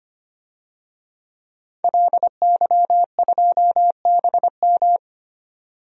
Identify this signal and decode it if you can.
Morse code, heard as a keyed tone, about 25 wpm, 700 Hz. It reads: LY2BM
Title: DXCC-100 Call Signs - 5 Characters - 25wpm